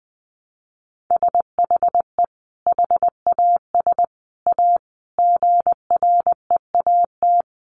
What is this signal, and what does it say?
Morse code, heard as a keyed tone, about 20 wpm, 700 Hz. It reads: SHE HAS A GREAT
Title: Sentences from Top 1100 Words 20wpm